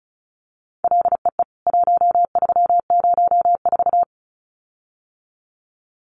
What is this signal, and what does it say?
Morse code, heard as a keyed tone, about 34 wpm, 700 Hz. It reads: LEE 1304
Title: CWT Exchange 35wpm